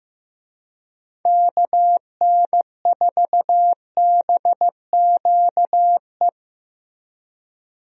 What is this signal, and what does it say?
Morse code, heard as a keyed tone, about 15 wpm, 700 Hz. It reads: KN4BQE
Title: USA Call Signs - 6 Characters - Repeated 2x - 15wpm